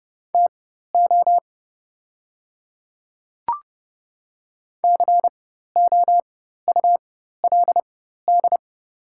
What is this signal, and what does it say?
Morse code, heard as a keyed tone, about 31 wpm, spaced as for 14 wpm, 700 Hz. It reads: TO COULD
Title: 4x Character Spacing - Top 100 Words - 30wpm